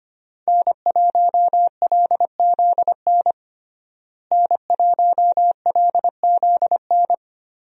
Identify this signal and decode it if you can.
Morse code, heard as a keyed tone, about 25 wpm, 700 Hz. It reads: N1LZN N1LZN
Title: USA Call Signs - 5 Characters - Repeated 3x - 25wpm